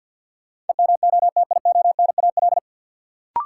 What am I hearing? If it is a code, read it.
Morse code, heard as a keyed tone, about 48 wpm, 700 Hz. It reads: EMOTIONAL
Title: Sentences from Top 1000 Words - Review Part 5 - New Word Drill - 50wpm